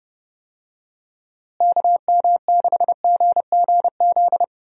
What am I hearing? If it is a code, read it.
Morse code, heard as a keyed tone, about 32 wpm, 700 Hz. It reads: KM6GGZ
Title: USA Call Signs - 6 Characters - Repeated 2x - 30wpm